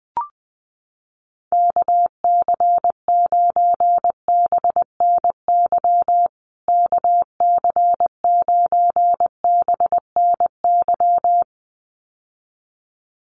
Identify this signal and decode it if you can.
Morse code, heard as a keyed tone, about 20 wpm, 700 Hz. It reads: KC9BNY KC9BNY
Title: USA Call Signs Repeated 2x 20wpm